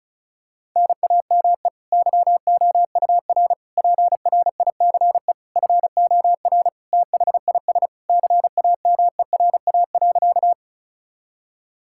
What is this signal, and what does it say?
Morse code, heard as a keyed tone, about 36 wpm, 700 Hz. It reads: NAME YOUR PRICE FOR THIS CAMERA.
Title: Sentences from Top 1000 Words - Review Part 5 - 35wpm